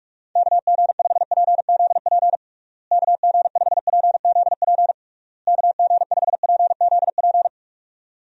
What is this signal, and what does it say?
Morse code, heard as a keyed tone, about 46 wpm, 700 Hz. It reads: KG5PZP KG5PZP KG5PZP
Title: USA Call Signs - 6 Characters - Repeated 3x - 45wpm